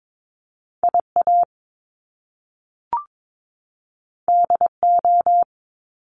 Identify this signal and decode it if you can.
Morse code, heard as a keyed tone, about 22 wpm, 700 Hz. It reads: IA DO
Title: Top Two Letter Combinations in Words 22wpm